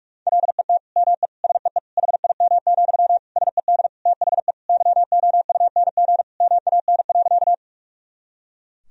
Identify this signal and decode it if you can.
Morse code, heard as a keyed tone, about 41 wpm, 700 Hz. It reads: LET ME SEE HIM, SED THE YOUNG MAN.
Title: Sentences from Top 400 Words and Abbreviations 45wpm